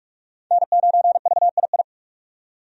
Morse code, heard as a keyed tone, about 46 wpm, 700 Hz. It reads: N9VII
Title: USA Call Signs - 5 Characters - 45wpm